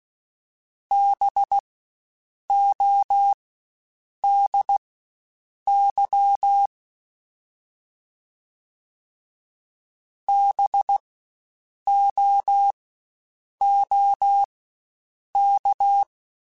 Morse code, heard as a keyed tone, about 16 wpm, spaced as for 7 wpm, 800 Hz. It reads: BODY BOOK
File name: Spaced Alphas / Spaced Alphas mp3s / Bs sp @15